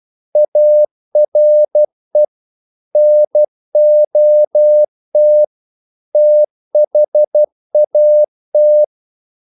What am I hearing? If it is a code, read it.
Morse code, heard as a keyed tone, about 12 wpm, 600 Hz. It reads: ARE NOT THAT
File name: CW / Practice / EnglishQuotations0112